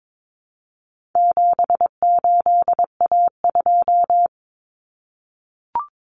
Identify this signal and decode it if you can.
Morse code, heard as a keyed tone, about 22 wpm, 700 Hz. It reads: 78A2
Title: Common Cut Numbers - 4-Digit 22wpm